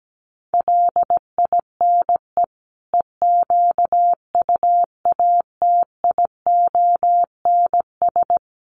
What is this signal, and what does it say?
Morse code, heard as a keyed tone, about 17 wpm, 700 Hz. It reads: LINE EQUATIONS
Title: Sets of 5 Words 17wpm